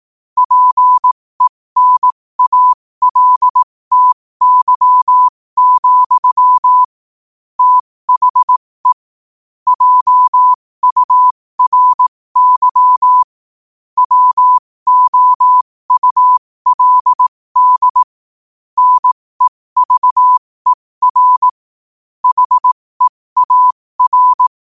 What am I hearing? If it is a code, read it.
Morse code, heard as a keyed tone, about 18 wpm, spaced as for 15 wpm, 1000 Hz. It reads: PENALTY, THE JURY WOULD NEVER HEAR
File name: qotdfiles / 15wpm